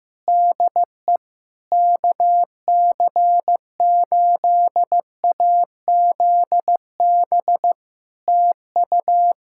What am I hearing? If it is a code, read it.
Morse code, heard as a keyed tone, about 15 wpm, 700 Hz. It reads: DE KC8AZB TU
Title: Practice QSO - Exchange 4 - Ending - 15wpm